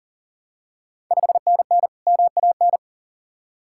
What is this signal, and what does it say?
Morse code, heard as a keyed tone, about 39 wpm, 690 Hz. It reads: 5NN MAN
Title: Ontario QSO Party Exchange 40wpm